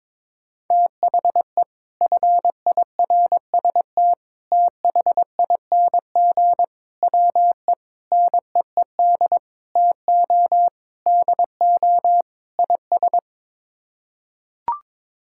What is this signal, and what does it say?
Morse code, heard as a keyed tone, about 22 wpm, 700 Hz. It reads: THE FIRST THING WE NEED TO DO IS
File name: practice / 22wpm / Sets of 8 Words - 22wpm